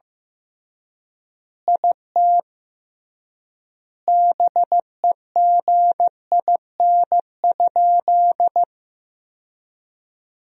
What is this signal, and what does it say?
Morse code, heard as a keyed tone, about 15 wpm, 700 Hz. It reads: IT BEGIN?
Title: 3x Word-Spacing - Sentences from Top 300 Words - 15wpm